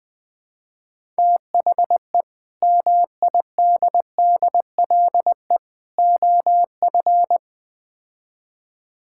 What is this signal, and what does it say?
Morse code, heard as a keyed tone, about 20 wpm, 700 Hz. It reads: THE MIDDLE OF
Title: Build A Phrase - 5-Word Phrases - 20wpm